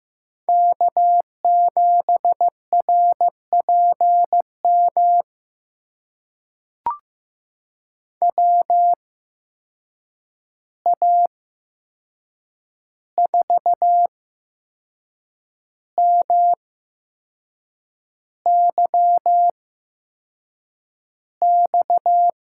Morse code, heard as a keyed tone, about 16 wpm, spaced as for 6 wpm, 700 Hz. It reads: K7RPM WA4MYX
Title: USA Call Signs - 8x Character Spacing - 15wpm